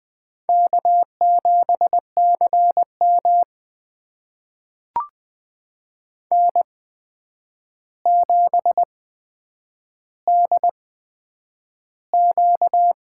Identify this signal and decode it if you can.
Morse code, heard as a keyed tone, about 20 wpm, spaced as for 9 wpm, 700 Hz. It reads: K7CM N7DQ
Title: USA Call Signs - 4 Characters - 8x Character Spacing - 20wpm